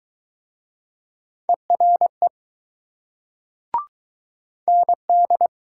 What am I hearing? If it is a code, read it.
Morse code, heard as a keyed tone, about 20 wpm, 700 Hz. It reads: ERE ND